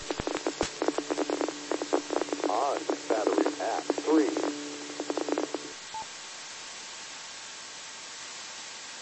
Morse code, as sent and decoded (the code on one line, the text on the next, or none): none